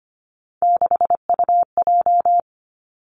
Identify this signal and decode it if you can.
Morse code, heard as a keyed tone, about 25 wpm, 700 Hz. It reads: 6UJ
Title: Three Challenging Letter-Number Combinations 25wpm